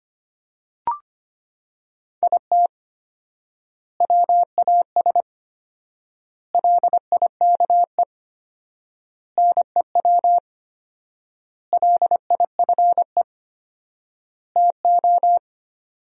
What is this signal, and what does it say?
Morse code, heard as a keyed tone, about 25 wpm, 700 Hz. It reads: IT WAS LIKE NEW LIFE TO
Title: 4x Word-Spacing - Sentences from Top 300 Words - 25wpm